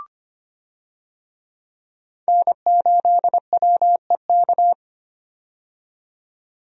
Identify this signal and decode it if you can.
Morse code, heard as a keyed tone, about 25 wpm, 690 Hz. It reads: N8WEK
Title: USA Call Signs - 5 Characters - 25wpm